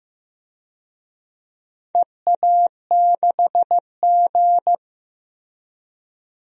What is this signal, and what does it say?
Morse code, heard as a keyed tone, about 15 wpm, 700 Hz. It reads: EA6G
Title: DXCC-100 Call Signs - 4 Characters - Repeated 2x - 15wpm